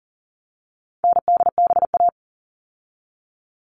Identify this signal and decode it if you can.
Morse code, heard as a keyed tone, about 40 wpm, 700 Hz. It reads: NDBA
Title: All Cut Numbers - 4-Digit 40wpm